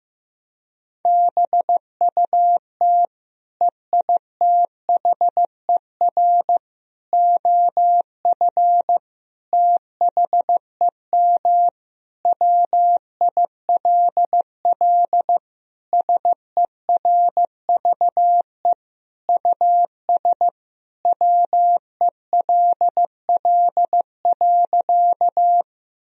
Morse code, heard as a keyed tone, about 15 wpm, 700 Hz. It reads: BUT EITHER OF THEM WILL SERVE US WELL.
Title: Sentences from Top 1000 Words - Review Part 4 - 15wpm